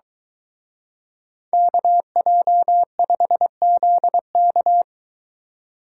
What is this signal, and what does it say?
Morse code, heard as a keyed tone, about 24 wpm, 700 Hz. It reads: KJ5ZK